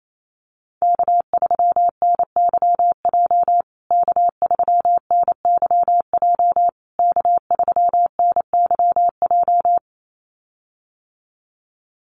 Morse code, heard as a keyed tone, about 28 wpm, 700 Hz. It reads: K3NYJ K3NYJ K3NYJ
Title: USA Call Signs Repeated 3x 28wpm